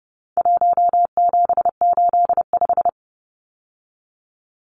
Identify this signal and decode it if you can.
Morse code, heard as a keyed tone, about 29 wpm, 700 Hz. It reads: 1785
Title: Numbers - 4-Digit 30wpm